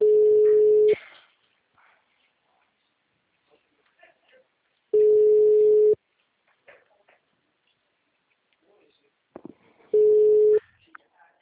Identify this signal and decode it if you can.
Ringback tone (Europe, 425 Hz), with speech.